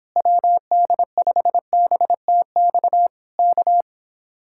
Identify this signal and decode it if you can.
Morse code, heard as a keyed tone, about 26 wpm, 700 Hz. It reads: WD5BTX K